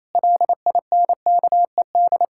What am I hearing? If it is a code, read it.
Morse code, heard as a keyed tone, about 27 wpm, 700 Hz. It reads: LINKED
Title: Pangram Challenge v2 - 28wpm